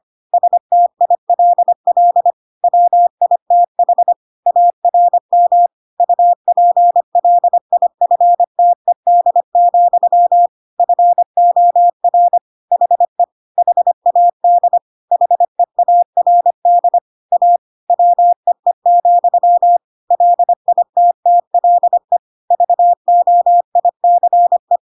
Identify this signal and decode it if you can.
Morse code, heard as a keyed tone, about 25 wpm, 700 Hz. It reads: STILL WITH ARM UPLIFTED, FOR HE HAD HEARD A WEE, LITTLE VOICE
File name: Pinocchio / Pinocchio ch1 25 25 ik1hge